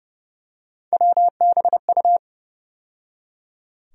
Morse code, heard as a keyed tone, about 32 wpm, 700 Hz. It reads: WBU
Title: USA Call Sign Postfixes - 3 Letters 30wpm